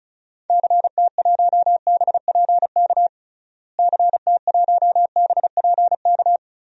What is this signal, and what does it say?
Morse code, heard as a keyed tone, about 36 wpm, 700 Hz. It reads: CT1BPK CT1BPK
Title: DXCC-100 Call Signs - 6 Characters - Repeated 3x - 35wpm